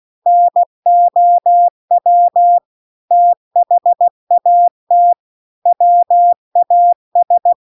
Morse code, heard as a keyed tone, about 16 wpm, 700 Hz. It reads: NOW THAT WAS